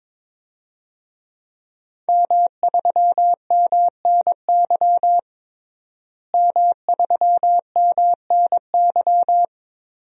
Morse code, heard as a keyed tone, about 22 wpm, 700 Hz. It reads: M3MNY M3MNY
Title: DXCC-100 Call Signs - 5 Characters - Repeated 3x - 22wpm